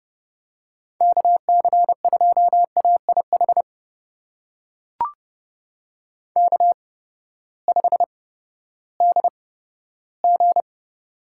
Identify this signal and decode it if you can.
Morse code, heard as a keyed tone, about 30 wpm, spaced as for 15 wpm, 700 Hz. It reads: KC2AIH K5DG
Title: USA Call Signs - 8x Character Spacing - 30wpm